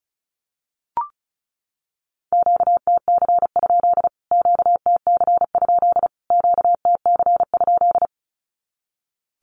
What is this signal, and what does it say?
Morse code, heard as a keyed tone, about 34 wpm, 700 Hz. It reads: QTC? QTC? QTC?
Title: Q-Codes Repeated 3x 35wpm